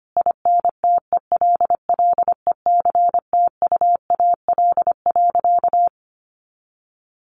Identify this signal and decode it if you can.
Morse code, heard as a keyed tone, about 25 wpm, 700 Hz. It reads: INTELLECTUAL.
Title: A Sound Of Thunder 25wpm